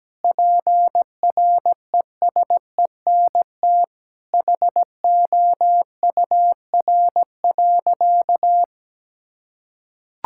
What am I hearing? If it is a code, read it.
Morse code, heard as a keyed tone, about 17 wpm, 700 Hz. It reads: PRESENT HOUR.
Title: Sentences from Top 1000 Words - Review Part 5 - 17wpm